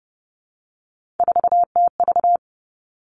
Morse code, heard as a keyed tone, about 30 wpm, 700 Hz. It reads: 4TV
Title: Three Challenging Letter-Number Combinations 30wpm